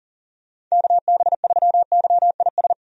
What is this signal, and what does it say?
Morse code, heard as a keyed tone, about 40 wpm, 700 Hz. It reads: KB3YIS
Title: USA Call Signs - 6 Characters - 40wpm